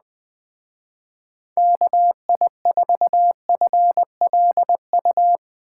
Morse code, heard as a keyed tone, about 20 wpm, 700 Hz. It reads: KI4FLU
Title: Speed Racing - USA Call Signs 17wpm